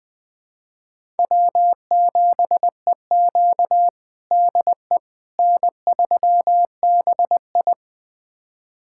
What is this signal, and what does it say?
Morse code, heard as a keyed tone, about 21 wpm, 700 Hz. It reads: W7EQ DE N3BI
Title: USA Callsign DE Callsign - 4 Characters - 20wpm